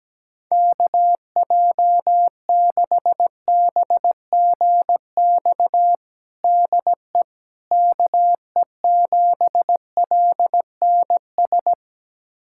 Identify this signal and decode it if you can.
Morse code, heard as a keyed tone, about 17 wpm, 700 Hz. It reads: KJ6BGX DE KE7LNS
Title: USA Callsign DE Callsign - 6 Characters - 17wpm